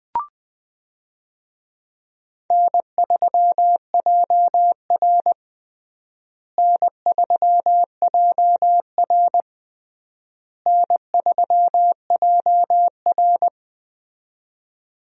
Morse code, heard as a keyed tone, about 20 wpm, 700 Hz. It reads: N3JR N3JR N3JR
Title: USA Call Signs - 4 Characters - Repeated 3x - 20wpm